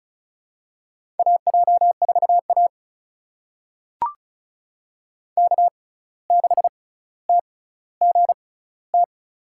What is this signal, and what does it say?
Morse code, heard as a keyed tone, about 36 wpm, spaced as for 17 wpm, 700 Hz. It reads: AJ4A K6TGT
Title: USA Call Signs - 6x Character Spacing - 35wpm